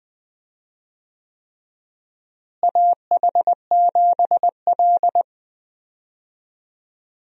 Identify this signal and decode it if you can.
Morse code, heard as a keyed tone, about 20 wpm, 700 Hz. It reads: AH7L
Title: USA Call Signs - 4 Characters - Repeated 3x - 20wpm